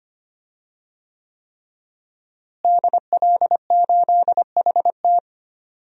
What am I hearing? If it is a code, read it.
Morse code, heard as a keyed tone, about 25 wpm, 700 Hz. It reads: DL8HT
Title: DXCC-100 Call Signs - 5 Characters - Repeated 3x - 25wpm